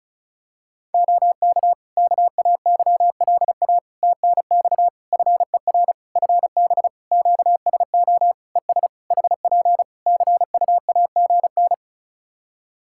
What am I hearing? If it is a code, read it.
Morse code, heard as a keyed tone, about 36 wpm, 700 Hz. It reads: OK KAYLA TNX FER FB QSO ES HP CUAGN
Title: K1USN SST QSO Practice 35wpm